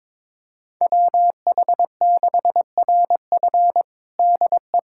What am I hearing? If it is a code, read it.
Morse code, heard as a keyed tone, about 22 wpm, 700 Hz. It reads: WH6RF DE
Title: USA Callsign DE Callsign - 5 Characters - 22wpm